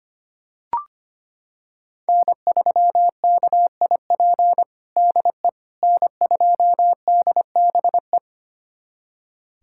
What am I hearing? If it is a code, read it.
Morse code, heard as a keyed tone, about 25 wpm, 700 Hz. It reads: N3KIP DE N2DBE
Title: USA Callsign DE Callsign - 5 Characters - 25wpm